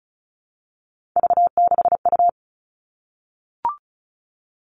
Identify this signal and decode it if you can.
Morse code, heard as a keyed tone, about 36 wpm, 700 Hz. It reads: V6U
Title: Three Challenging Letter-Number Combinations 35wpm